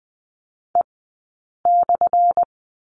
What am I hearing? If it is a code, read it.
Morse code, heard as a keyed tone, about 20 wpm, spaced as for 8 wpm, 700 Hz. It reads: E/
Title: 177 Lesson 29d - Slash - Two Characters 20wpm